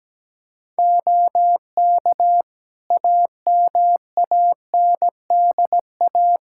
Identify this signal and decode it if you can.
Morse code, heard as a keyed tone, about 17 wpm, 700 Hz. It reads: OK AMANDA
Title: K1USN SST QSO Practice 17wpm